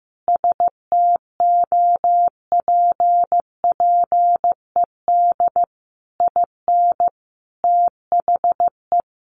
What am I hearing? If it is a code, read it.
Morse code, heard as a keyed tone, about 15 wpm, 700 Hz. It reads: STOPPED IN THE
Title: A Sound Of Thunder 15wpm